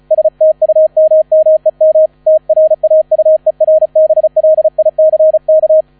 Morse code, heard as a keyed tone, about 34 wpm, 630 Hz, with radio noise. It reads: STUMMEM TRAUERBLICK
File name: 35wpm